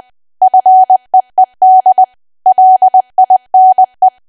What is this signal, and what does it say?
Morse code, heard as a keyed tone, about 20 wpm, 750 Hz. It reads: FEED LINE
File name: cwa / W1AW-Level3 / 20WPM